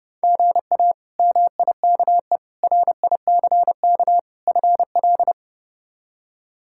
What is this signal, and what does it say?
Morse code, heard as a keyed tone, about 32 wpm, 700 Hz. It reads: GA MIKE RICK FL
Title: K1USN SST QSO Practice - 30wpm